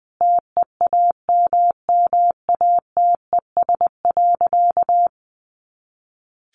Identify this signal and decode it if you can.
Morse code, heard as a keyed tone, about 19 wpm, 700 Hz. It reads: TEAMMATES.
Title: Easy Phrases 20wpm